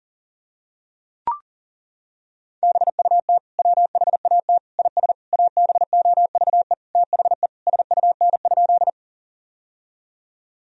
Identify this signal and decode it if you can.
Morse code, heard as a keyed tone, about 40 wpm, 700 Hz. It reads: BUT WHAT IS ABOVE THE SUN?
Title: Sentences from Top 1000 Words - Review Part 2 - 40wpm